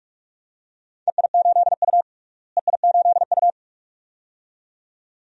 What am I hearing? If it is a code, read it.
Morse code, heard as a keyed tone, about 41 wpm, 700 Hz. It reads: EI8U EI8U
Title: DXCC-100 Call Signs - 4 Characters - Repeated 2x - 45wpm